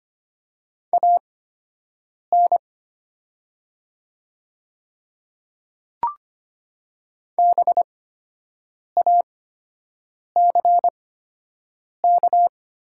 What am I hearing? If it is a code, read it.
Morse code, heard as a keyed tone, about 25 wpm, spaced as for 7 wpm, 700 Hz. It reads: AN BACK
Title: 8x Character Spacing - Top 200 Words - 25wpm